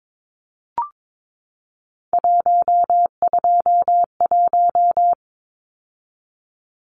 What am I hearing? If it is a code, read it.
Morse code, heard as a keyed tone, about 22 wpm, 700 Hz. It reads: 121
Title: Numbers - 3-Digit 22wpm